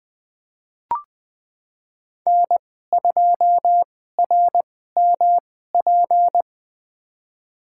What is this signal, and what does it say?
Morse code, heard as a keyed tone, about 20 wpm, spaced as for 14 wpm, 700 Hz. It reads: N2RMP
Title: USA Call Signs - 5 Characters - 2x Character Spacing - 20wpm